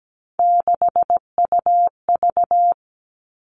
Morse code, heard as a keyed tone, about 17 wpm, 700 Hz. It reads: 6UV
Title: All Cut Numbers - 3-Digit 17wpm